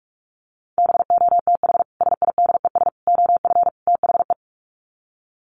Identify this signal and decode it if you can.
Morse code, heard as a keyed tone, about 44 wpm, 700 Hz. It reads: BOTH SIDES OF THE
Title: Sets of 4 Words 45wpm